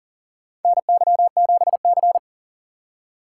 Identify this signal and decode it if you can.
Morse code, heard as a keyed tone, about 37 wpm, 700 Hz. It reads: NY7C
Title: USA Call Signs - 4 Characters - 2x Character Spacing - 40wpm